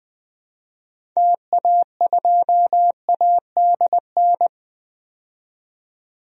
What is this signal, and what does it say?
Morse code, heard as a keyed tone, about 20 wpm, 700 Hz. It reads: TA2ADN
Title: DXCC-100 Call Signs - 6 Characters - 20wpm